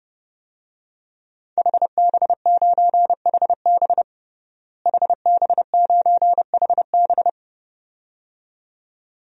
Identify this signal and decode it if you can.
Morse code, heard as a keyed tone, about 30 wpm, 700 Hz. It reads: HB9HB HB9HB
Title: DXCC-100 Call Signs - 5 Characters - Repeated 2x - 30wpm